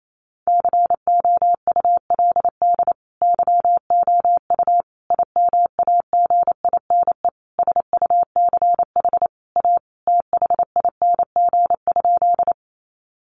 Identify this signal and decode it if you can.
Morse code, heard as a keyed tone, about 28 wpm, 700 Hz. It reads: COULD YOU IMAGINE SUCH A THING?
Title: Sentences from Top 1400 Words 28wpm